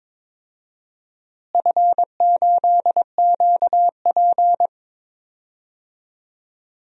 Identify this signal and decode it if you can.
Morse code, heard as a keyed tone, about 22 wpm, 700 Hz. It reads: F8QP
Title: DXCC-100 Call Signs - 4 Characters - Repeated 3x - 22wpm